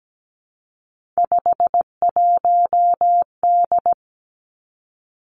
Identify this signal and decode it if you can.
Morse code, heard as a keyed tone, about 17 wpm, 700 Hz. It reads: 51D
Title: Three Challenging Letter-Number Combinations 17wpm